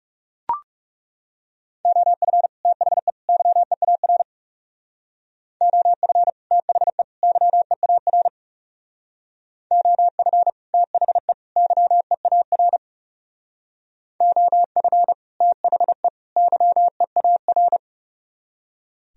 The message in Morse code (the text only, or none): OF THE YEAR OF THE YEAR OF THE YEAR OF THE YEAR